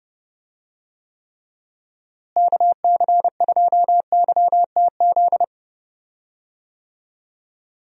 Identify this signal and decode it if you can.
Morse code, heard as a keyed tone, about 30 wpm, 700 Hz. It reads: KC2YTZ
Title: USA Call Signs - 6 Characters - Repeated 2x - 30wpm